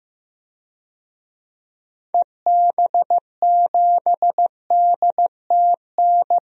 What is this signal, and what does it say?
Morse code, heard as a keyed tone, about 15 wpm, 700 Hz. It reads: EB7DTN
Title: DXCC-100 Call Signs - 6 Characters - Repeated 2x - 15wpm